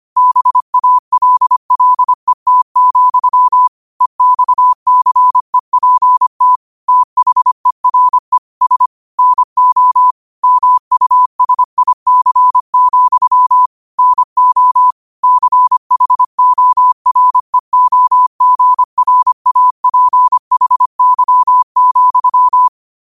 Morse code, heard as a keyed tone, about 25 wpm, 1000 Hz. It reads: BALLET, EXCEPT THERE S NO MUSIC, NO CHOREOGRAPHY,